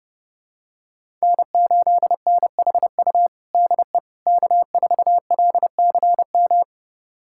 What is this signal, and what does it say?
Morse code, heard as a keyed tone, about 30 wpm, 700 Hz. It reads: N8NHU DE K4LCM
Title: USA Callsign DE Callsign - 5 Characters - 30wpm